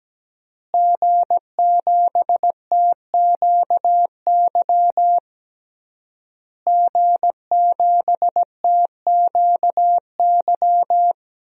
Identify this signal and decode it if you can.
Morse code, heard as a keyed tone, about 17 wpm, 700 Hz. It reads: G7TQY G7TQY
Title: DXCC-100 Call Signs - 5 Characters - Repeated 3x - 17wpm